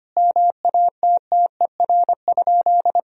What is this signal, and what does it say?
Morse code, heard as a keyed tone, about 25 wpm, 700 Hz. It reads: MATTER?
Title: Sentences from Top 1000 Words - Review Part 4 - 25wpm